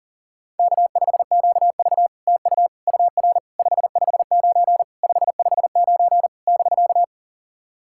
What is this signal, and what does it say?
Morse code, heard as a keyed tone, about 44 wpm, 700 Hz. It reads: K5QV TU UR 559 559 <BK>
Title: SOTA QSO Practice 40wpm